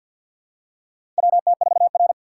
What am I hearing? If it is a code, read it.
Morse code, heard as a keyed tone, about 52 wpm, 700 Hz. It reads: WT4R
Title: USA Call Signs - 4 Characters - 6x Character Spacing - 50wpm